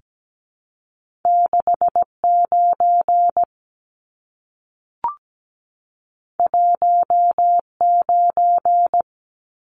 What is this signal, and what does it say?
Morse code, heard as a keyed tone, about 17 wpm, 700 Hz. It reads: 69 19
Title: Numbers - 2-Digit 17wpm